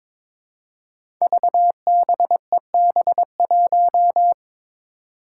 Morse code, heard as a keyed tone, about 22 wpm, 700 Hz. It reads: VBEB1
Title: Five Challenging Letter-Number Combinations 22wpm